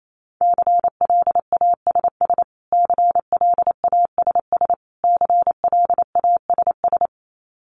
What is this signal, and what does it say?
Morse code, heard as a keyed tone, about 28 wpm, 700 Hz. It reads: CLASS CLASS CLASS
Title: Top 500 Words Repeated 28wpm